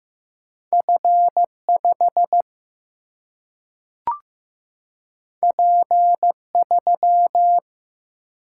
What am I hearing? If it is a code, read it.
Morse code, heard as a keyed tone, about 15 wpm, 700 Hz. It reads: F5 P3
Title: Single-Letter Single-Number 15wpm